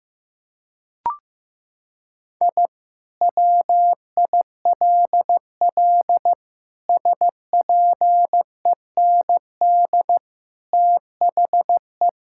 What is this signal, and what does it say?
Morse code, heard as a keyed tone, about 15 wpm, 700 Hz. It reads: I WILL SPEND THE
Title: Sentences from Top 1000 Words - Review Part 5 - 15wpm